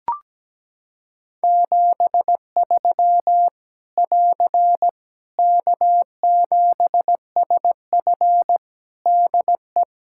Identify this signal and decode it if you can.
Morse code, heard as a keyed tone, about 17 wpm, 700 Hz. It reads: 73 <AR> K7SF DE
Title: Practice QSO - Exchange 4 - Ending - 17wpm